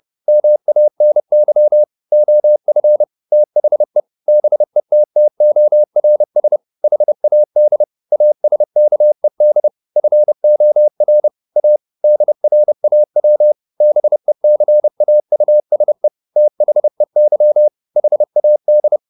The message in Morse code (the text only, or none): MANY OF THE BETTORS HAD ASKED FOR A DRAW BECAUSE THEY HAD